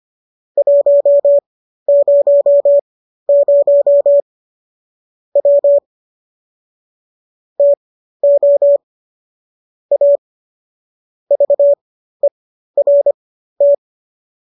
Morse code, heard as a keyed tone, about 25 wpm, spaced as for 13 wpm, 570 Hz. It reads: 100 W TO A VERT